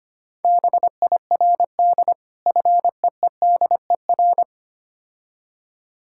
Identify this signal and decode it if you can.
Morse code, heard as a keyed tone, about 25 wpm, 690 Hz. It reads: BIRD FEEDER
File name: practice / 25wpm / Difficult Words with Hints - 25wpm